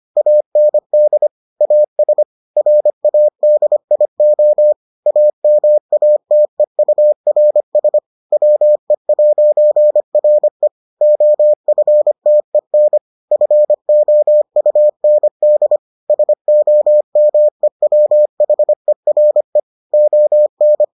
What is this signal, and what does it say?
Morse code, heard as a keyed tone, about 25 wpm, 600 Hz. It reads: AND AS RADIO AMATEURS WE'RE OFTEN FOUND SOMEWHERE ON